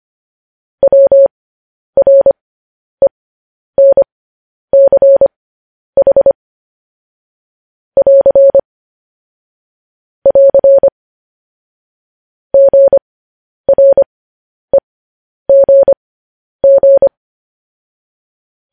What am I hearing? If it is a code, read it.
Morse code, heard as a keyed tone, about 25 wpm, spaced as for 10 wpm, 550 Hz. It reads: WRENCH <AR> <AR> GREGG
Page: News Headlines in Morse Code at 10 WPM